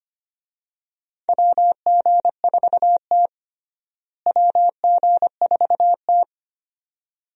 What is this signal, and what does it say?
Morse code, heard as a keyed tone, about 25 wpm, 700 Hz. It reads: WG4T WG4T
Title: USA Call Signs - 4 Characters - Repeated 2x - 25wpm